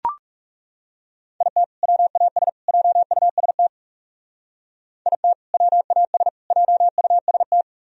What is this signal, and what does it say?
Morse code, heard as a keyed tone, about 35 wpm, 700 Hz. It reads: IT WAS JUST IT WAS JUST
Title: Speed-Racing - Sets of 3 Words 30wpm